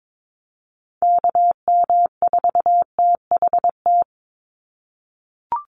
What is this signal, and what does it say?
Morse code, heard as a keyed tone, about 22 wpm, 700 Hz. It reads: KM4THT
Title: USA Call Signs 22wpm